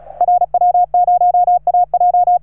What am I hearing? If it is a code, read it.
Morse code, heard as a keyed tone, about 36 wpm, 710 Hz. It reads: RW0AJ
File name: RS3A-20